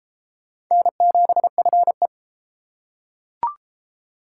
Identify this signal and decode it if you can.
Morse code, heard as a keyed tone, about 28 wpm, 700 Hz. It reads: N7FE